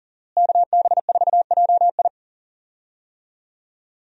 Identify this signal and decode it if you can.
Morse code, heard as a keyed tone, about 40 wpm, 700 Hz. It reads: KB4JI
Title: USA Call Signs - 5 Characters - Repeated 2x - 40wpm